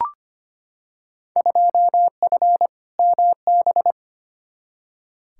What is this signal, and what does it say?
Morse code, heard as a keyed tone, about 25 wpm, 700 Hz. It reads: 2F MB
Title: Field Day Exchanges Encore - 25wpm